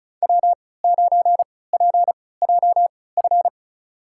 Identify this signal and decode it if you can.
Morse code, heard as a keyed tone, about 34 wpm, spaced as for 20 wpm, 700 Hz. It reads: W9PJF
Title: USA Call Signs - 5 Characters - 3x Character Spacing - 35wpm